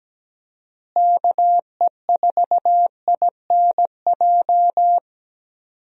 Morse code, heard as a keyed tone, about 17 wpm, 700 Hz. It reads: KE4INJ
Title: USA Call Signs - 6 Characters - Repeated 3x - 17wpm